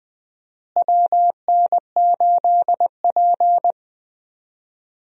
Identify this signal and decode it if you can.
Morse code, heard as a keyed tone, about 20 wpm, 700 Hz. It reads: WN8P
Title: USA Call Signs - 4 Characters - Repeated 3x - 20wpm